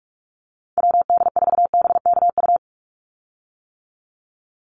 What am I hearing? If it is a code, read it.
Morse code, heard as a keyed tone, about 46 wpm, 690 Hz. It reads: WD4BKU
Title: USA Call Signs 45wpm